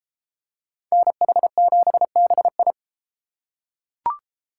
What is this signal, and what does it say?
Morse code, heard as a keyed tone, about 32 wpm, 700 Hz. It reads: NH7BI